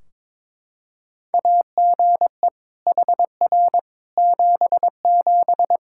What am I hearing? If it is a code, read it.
Morse code, heard as a keyed tone, about 22 wpm, 700 Hz. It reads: AGE HR 77
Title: Practice QSO - Exchange 3 - More Info - 22wpm